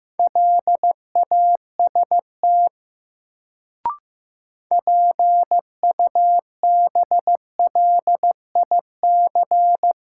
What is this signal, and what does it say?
Morse code, heard as a keyed tone, about 15 wpm, 700 Hz. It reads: LAST PUBLIC
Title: Sets of 2 Words - New Word Drill - 15wpm